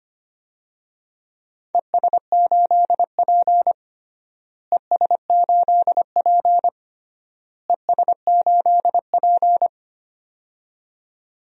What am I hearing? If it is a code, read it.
Morse code, heard as a keyed tone, about 25 wpm, 700 Hz. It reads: ES8P ES8P ES8P
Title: DXCC-100 Call Signs - 4 Characters - Repeated 3x - 25wpm